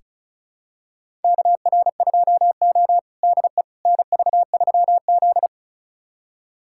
Morse code, heard as a keyed tone, about 36 wpm, 700 Hz. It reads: KR2O DE NV3Z
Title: USA Callsign DE Callsign - 4 Characters - 35wpm